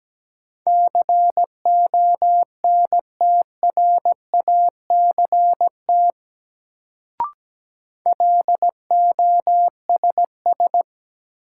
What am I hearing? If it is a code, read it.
Morse code, heard as a keyed tone, about 17 wpm, 700 Hz. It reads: CONTRACT LOSS
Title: Sentences from Top 1400 Words - New Word Drill Alt - 17wpm